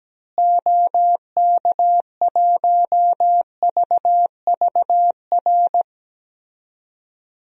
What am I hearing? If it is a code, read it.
Morse code, heard as a keyed tone, about 17 wpm, 700 Hz. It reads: OK1VVR
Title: DXCC-100 Call Signs - 6 Characters - Repeated 3x - 17wpm